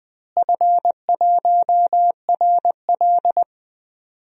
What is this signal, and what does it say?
Morse code, heard as a keyed tone, about 20 wpm, 700 Hz. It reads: F1RL
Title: DXCC-100 Call Signs - 4 Characters - Repeated 2x - 20wpm